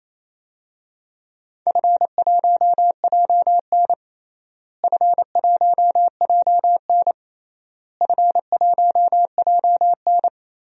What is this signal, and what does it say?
Morse code, heard as a keyed tone, about 28 wpm, 700 Hz. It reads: F1JN F1JN F1JN
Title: DXCC-100 Call Signs - 4 Characters - Repeated 3x - 28wpm